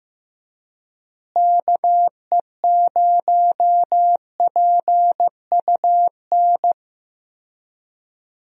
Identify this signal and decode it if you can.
Morse code, heard as a keyed tone, about 16 wpm, 700 Hz. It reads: KE0PUN
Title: USA Call Signs - 6 Characters - Repeated 2x - 15wpm